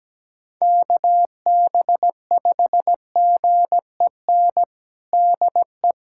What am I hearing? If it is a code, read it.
Morse code, heard as a keyed tone, about 17 wpm, 700 Hz. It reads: KB5GEN DE
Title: USA Callsign DE Callsign - 6 Characters - 17wpm